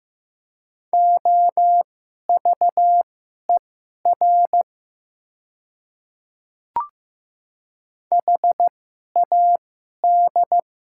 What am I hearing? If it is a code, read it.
Morse code, heard as a keyed tone, about 15 wpm, spaced as for 11 wpm, 700 Hz. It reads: OVER HAD
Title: 2x Character Spacing - Top 100 Words - 15wpm